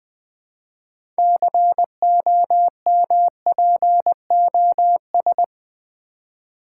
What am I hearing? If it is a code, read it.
Morse code, heard as a keyed tone, about 20 wpm, 700 Hz. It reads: COMPOS
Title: Build A Word - 2 Letters At A Time - 8-Letter Words - 20wpm